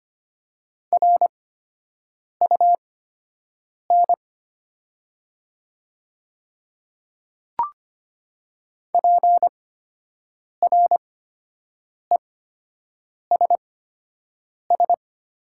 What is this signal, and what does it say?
Morse code, heard as a keyed tone, about 25 wpm, spaced as for 7 wpm, 700 Hz. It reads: RUN PRESS
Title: 8x Character Spacing - Top 300 Words - 25wpm